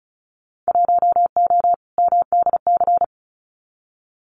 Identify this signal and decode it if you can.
Morse code, heard as a keyed tone, about 36 wpm, 700 Hz. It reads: 1O MDC
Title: Winter Field Day Exchanges 35wpm